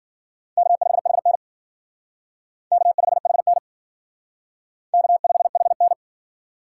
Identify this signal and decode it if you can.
Morse code, heard as a keyed tone, about 54 wpm, 700 Hz. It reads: K5HN K5HN K5HN